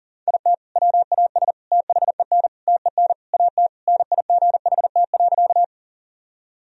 Morse code, heard as a keyed tone, about 40 wpm, 700 Hz. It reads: IT WAS THEN TEN AT NIGHT.
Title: Sentences from Top 1000 Words - Review Part 2 - 40wpm